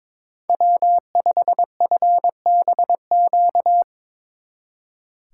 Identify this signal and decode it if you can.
Morse code, heard as a keyed tone, about 23 wpm, 700 Hz. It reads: W5FBQ
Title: Speed Racing - USA 1x3 Call Signs 22wpm